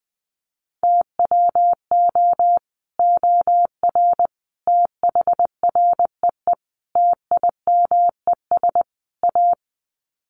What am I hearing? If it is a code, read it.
Morse code, heard as a keyed tone, about 20 wpm, 700 Hz. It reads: TWO OR THREE TIMES A
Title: Sets of 5 Words Encore2 20wpm